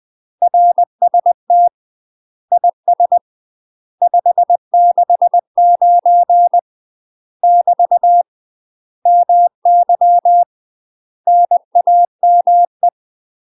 Morse code, heard as a keyed tone, about 20 wpm, 700 Hz. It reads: RST IS 569 <BT> MY NAME
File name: UK MASTER QSOs TRAINING / 20WPM / UK MASTER QSOs 0048